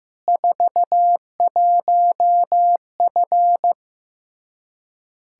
Morse code, heard as a keyed tone, about 16 wpm, 700 Hz. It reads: 41F
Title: K1USN SST QSO Practice 15wpm